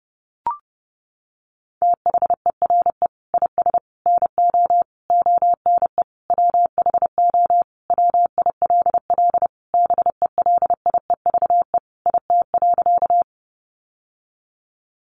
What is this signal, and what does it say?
Morse code, heard as a keyed tone, about 30 wpm, 700 Hz. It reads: THERE IS NO ONE WHO WILL BELIEVE IT.
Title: Sentences from Top 1000 Words 30wpm